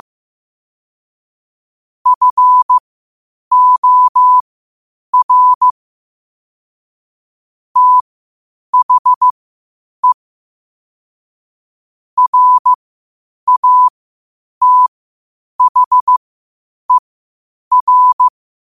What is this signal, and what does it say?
Morse code, heard as a keyed tone, about 15 wpm, spaced as for 8 wpm, 1000 Hz. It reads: FOR THE RATHER
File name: qotdfiles / 8wpm